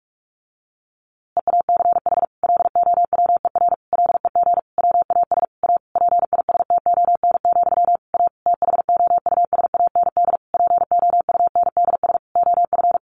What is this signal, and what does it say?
Morse code, heard as a keyed tone, about 46 wpm, 700 Hz. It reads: EACH LOWER LEG WAS A PISTON, A THOUSAND POUNDS OF
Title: A Sound Of Thunder 45wpm